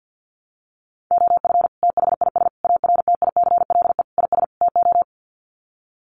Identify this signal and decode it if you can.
Morse code, heard as a keyed tone, about 50 wpm, 700 Hz. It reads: OF THIS ARTICLE IS TO
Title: Sets of 5 Words Encore2 50wpm